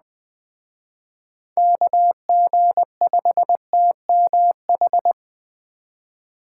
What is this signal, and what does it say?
Morse code, heard as a keyed tone, about 21 wpm, 700 Hz. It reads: KG5TMH